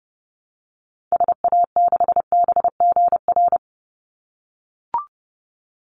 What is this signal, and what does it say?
Morse code, heard as a keyed tone, about 30 wpm, 700 Hz. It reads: SA6BGR
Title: Call Signs - 30wpm